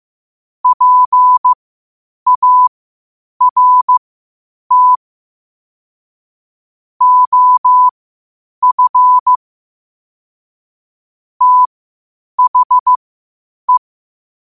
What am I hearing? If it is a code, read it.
Morse code, heard as a keyed tone, about 15 wpm, spaced as for 8 wpm, 1000 Hz. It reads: PART OF THE
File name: qotdfiles / 8wpm